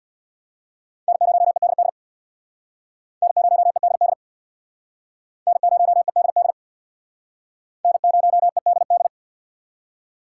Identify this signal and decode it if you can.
Morse code, heard as a keyed tone, about 67 wpm, 700 Hz. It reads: N0EDD N0EDD N0EDD N0EDD